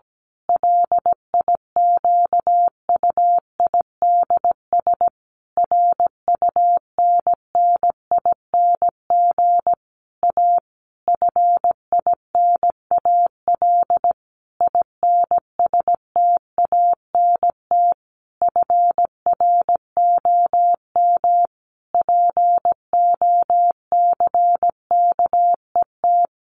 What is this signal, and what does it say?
Morse code, heard as a keyed tone, about 17 wpm, 700 Hz. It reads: LIQUIDS RUNNING A FINAL INSTANT FROM POCKET
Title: A Sound Of Thunder 17wpm